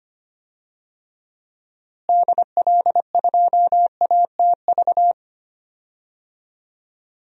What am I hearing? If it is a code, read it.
Morse code, heard as a keyed tone, about 25 wpm, 700 Hz. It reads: DL2ATV
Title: DXCC-100 Call Signs - 6 Characters - Repeated 3x - 25wpm